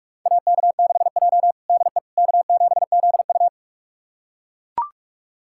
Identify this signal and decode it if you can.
Morse code, heard as a keyed tone, about 41 wpm, 700 Hz. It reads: AK6J DE K7ZU
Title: USA Callsign DE Callsign - 4 Characters - 45wpm